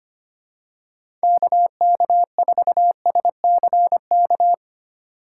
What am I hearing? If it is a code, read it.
Morse code, heard as a keyed tone, about 25 wpm, 700 Hz. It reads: KK4SCK
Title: USA Call Signs - 6 Characters - 2x Character Spacing - 25wpm